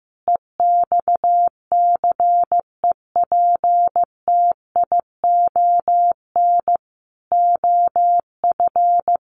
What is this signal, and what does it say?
Morse code, heard as a keyed tone, about 15 wpm, 700 Hz. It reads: EXCEPTION OF
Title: Sets of 4 Words 15wpm